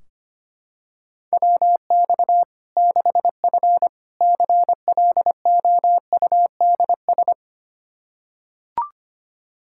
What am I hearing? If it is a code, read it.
Morse code, heard as a keyed tone, about 25 wpm, 700 Hz. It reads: WX 6F CLOUDS
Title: Practice QSO - Exchange 2 - Info - 25wpm